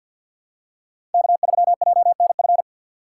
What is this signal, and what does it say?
Morse code, heard as a keyed tone, about 47 wpm, 700 Hz. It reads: K3JNF
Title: USA Call Signs - 5 Characters - 2x Character Spacing - 50wpm